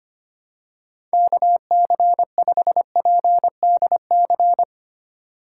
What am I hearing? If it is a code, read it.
Morse code, heard as a keyed tone, about 25 wpm, 700 Hz. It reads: KC5PDC
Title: USA Call Signs - 4x Character Spacing - 25wpm